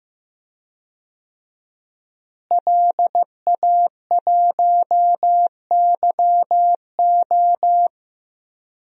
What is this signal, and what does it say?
Morse code, heard as a keyed tone, about 15 wpm, 700 Hz. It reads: LA1YO
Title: DXCC-100 Call Signs - 5 Characters - Repeated 3x - 15wpm